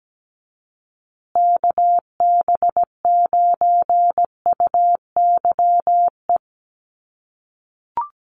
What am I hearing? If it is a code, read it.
Morse code, heard as a keyed tone, about 17 wpm, 700 Hz. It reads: KB9UYE
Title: USA Call Signs Repeated 3x 17wpm